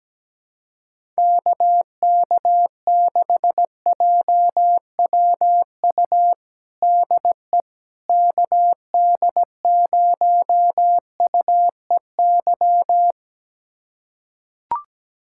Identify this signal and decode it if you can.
Morse code, heard as a keyed tone, about 17 wpm, 700 Hz. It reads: KK6JWU DE KD0UEY
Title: USA Callsign DE Callsign 17wpm